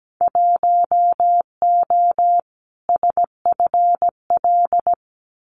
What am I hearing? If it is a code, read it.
Morse code, heard as a keyed tone, about 17 wpm, 690 Hz. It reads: 1O SFL
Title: Winter Field Day Exchanges 17wpm